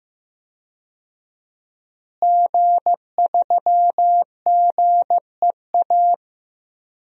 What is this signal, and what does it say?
Morse code, heard as a keyed tone, about 15 wpm, 700 Hz. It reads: G3GEA
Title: DXCC-100 Call Signs - 5 Characters - Repeated 3x - 15wpm